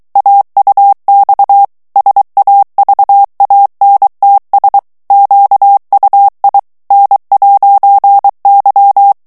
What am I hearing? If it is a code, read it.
Morse code, heard as a keyed tone, about 23 wpm, 800 Hz. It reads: AUX SAVANTS QUI N'Y
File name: morse mp3 / FRANCAIS TEXTE CLAIR 2 VITESSE 22 MOTS MINUTES